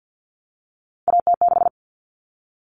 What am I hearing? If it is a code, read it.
Morse code, heard as a keyed tone, about 50 wpm, 700 Hz. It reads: AT6
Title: Common Cut Numbers - 3-Digit 50wpm